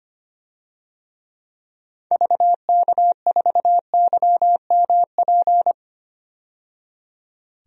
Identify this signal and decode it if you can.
Morse code, heard as a keyed tone, about 25 wpm, 700 Hz. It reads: VK4YMP
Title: DXCC-100 Call Signs - 6 Characters - 25wpm